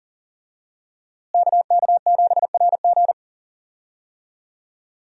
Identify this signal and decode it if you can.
Morse code, heard as a keyed tone, about 40 wpm, 700 Hz. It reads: KK7RG
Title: USA Call Signs - 5 Characters - Repeated 3x - 40wpm